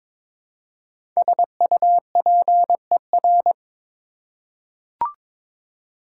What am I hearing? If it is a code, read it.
Morse code, heard as a keyed tone, about 22 wpm, 700 Hz. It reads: SUPER
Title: Prefixes and Suffixes 22wpm